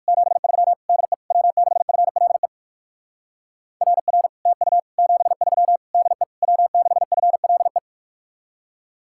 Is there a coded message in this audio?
Yes: Morse code from 0.1 to 7.8 s, heard as a keyed tone, about 58 wpm, 700 Hz. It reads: 73 DE W6FLE RR TU 73 DE W6FLE